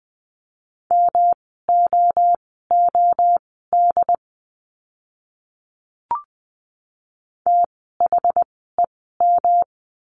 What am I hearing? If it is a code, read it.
Morse code, heard as a keyed tone, about 20 wpm, spaced as for 14 wpm, 700 Hz. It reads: MOOD THEM
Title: 108 Lesson 18f - Letter m - Words 20wpm